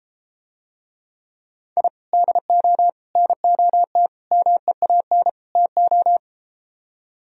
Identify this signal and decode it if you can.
Morse code, heard as a keyed tone, about 33 wpm, 700 Hz. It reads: I DO NOT MEAN TO